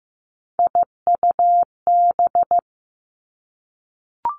Morse code, heard as a keyed tone, about 15 wpm, 700 Hz. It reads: IUB
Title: Three Challenging Letter-Number Combinations 15wpm